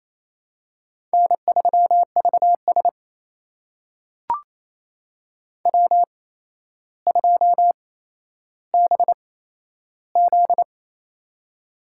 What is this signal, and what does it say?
Morse code, heard as a keyed tone, about 28 wpm, spaced as for 12 wpm, 700 Hz. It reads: N3VS W2BZ
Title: USA Call Signs - 4 Characters - 8x Character Spacing - 28wpm